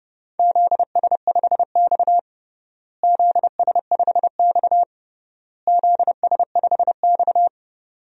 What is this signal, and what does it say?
Morse code, heard as a keyed tone, about 30 wpm, 700 Hz. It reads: ZS5X ZS5X ZS5X
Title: DXCC-100 Call Signs - 4 Characters - Repeated 3x - 30wpm